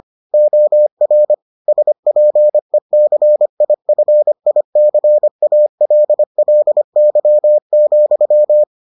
Morse code, heard as a keyed tone, about 25 wpm, 600 Hz. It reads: OR SPECIFICALLY,